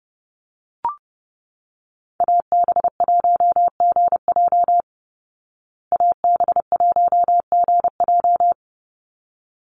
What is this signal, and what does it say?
Morse code, heard as a keyed tone, about 30 wpm, 700 Hz. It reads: AB1GJ AB1GJ
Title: Call Signs Repeated 3x - 30wpm